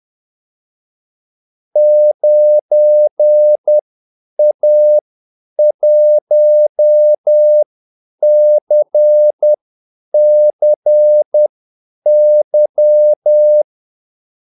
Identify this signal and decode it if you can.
Morse code, heard as a keyed tone, about 10 wpm, spaced as for 8 wpm, 600 Hz. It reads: 9A1CCY
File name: Callsigns-10-8wpm